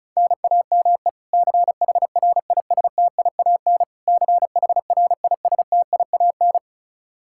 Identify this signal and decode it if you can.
Morse code, heard as a keyed tone, about 36 wpm, 700 Hz. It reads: NAME CHRISTIAN CHRISTIAN
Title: Practice QSO - Exchange 1 - Beginning - 35wpm